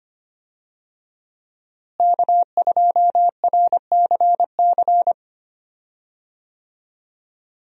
Morse code, heard as a keyed tone, about 25 wpm, 700 Hz. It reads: K2RCC
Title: USA Call Signs - 5 Characters - Repeated 2x - 25wpm